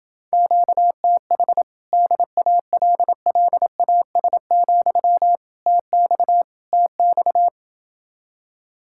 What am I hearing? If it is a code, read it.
Morse code, heard as a keyed tone, about 28 wpm, 700 Hz. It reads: QTH DALLAS, TX TX